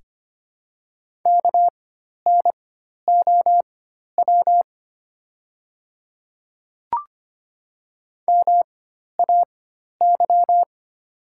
Morse code, heard as a keyed tone, about 25 wpm, spaced as for 12 wpm, 700 Hz. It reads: KNOW MAY
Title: 4x Character Spacing - Top 100 Words - 25wpm